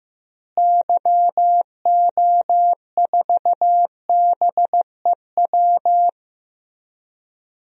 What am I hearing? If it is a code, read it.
Morse code, heard as a keyed tone, about 15 wpm, 700 Hz. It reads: YO4BEW
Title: DXCC-100 Call Signs - 6 Characters - Repeated 3x - 15wpm